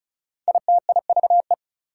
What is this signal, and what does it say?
Morse code, heard as a keyed tone, about 36 wpm, 700 Hz. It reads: ITIVE
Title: Prefixes and Suffixes 35wpm